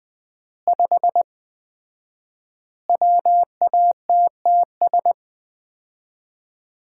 Morse code, heard as a keyed tone, about 20 wpm, 700 Hz. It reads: 5 WATTS
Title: Common QSO Phrases - 4x Word-Spacing - 20wpm